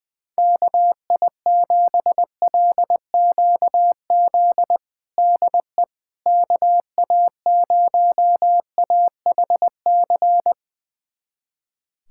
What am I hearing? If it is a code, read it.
Morse code, heard as a keyed tone, about 20 wpm, 700 Hz. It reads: KI7LQZ DE KA0AHC
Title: USA Callsign DE Callsign 20wpm